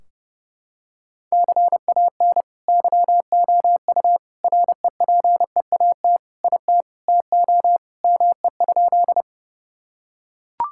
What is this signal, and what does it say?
Morse code, heard as a keyed tone, about 30 wpm, 700 Hz. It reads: CAN YOU REPEAT IT TO ME?
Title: Sentences from Top 1600 Words 30wpm